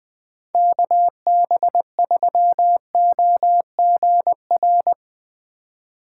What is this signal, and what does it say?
Morse code, heard as a keyed tone, about 22 wpm, 700 Hz. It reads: KB3OGR
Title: Speed Racing - USA 2x3 Call Signs 20wpm